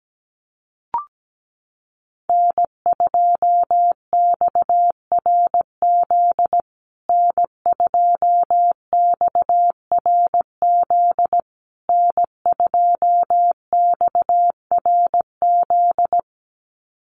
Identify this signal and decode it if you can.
Morse code, heard as a keyed tone, about 17 wpm, 700 Hz. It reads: N2XRZ N2XRZ N2XRZ
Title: USA Call Signs Repeated 3x 17wpm